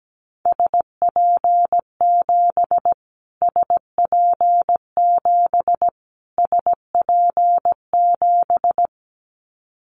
Morse code, heard as a keyed tone, about 17 wpm, 700 Hz. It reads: SP7 SP7 SP7
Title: Callsign Prefixes for Top 25 DXCC - Repeated 3x - 17wpm